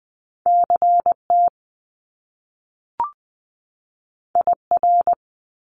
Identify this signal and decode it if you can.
Morse code, heard as a keyed tone, about 20 wpm, 700 Hz. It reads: CT IR
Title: Top Two Letter Combinations in Words 20wpm